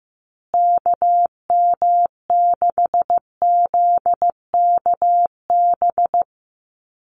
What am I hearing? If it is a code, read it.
Morse code, heard as a keyed tone, about 15 wpm, 700 Hz. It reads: KM6ZKB
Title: USA Call Signs Repeated 2x 15wpm